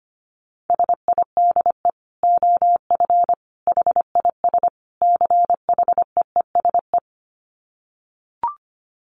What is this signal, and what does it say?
Morse code, heard as a keyed tone, about 25 wpm, 700 Hz. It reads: SIDE OF HIS CHEESE
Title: Dit Challenge 25wpm